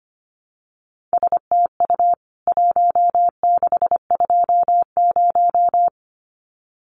Morse code, heard as a keyed tone, about 25 wpm, 700 Hz. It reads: STU 1620
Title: CWT Exchange 25wpm